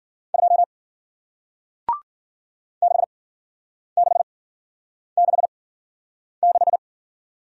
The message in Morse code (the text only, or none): 2 6666